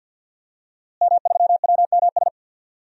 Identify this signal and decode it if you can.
Morse code, heard as a keyed tone, about 46 wpm, 700 Hz. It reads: M3WMS
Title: DXCC-100 Call Signs - 5 Characters - Repeated 3x - 50wpm